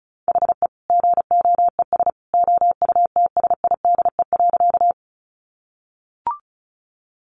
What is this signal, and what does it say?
Morse code, heard as a keyed tone, about 34 wpm, 700 Hz. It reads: HE GOES OUTSIDE.
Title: Additional Easy Phrases 35wpm